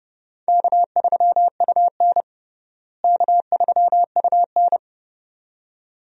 Morse code, heard as a keyed tone, about 31 wpm, 700 Hz. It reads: K3UN K3UN
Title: USA Call Signs - 4 Characters - Repeated 2x - 30wpm